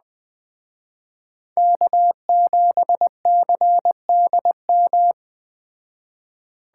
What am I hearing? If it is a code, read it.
Morse code, heard as a keyed tone, about 19 wpm, 700 Hz. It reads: K7CDM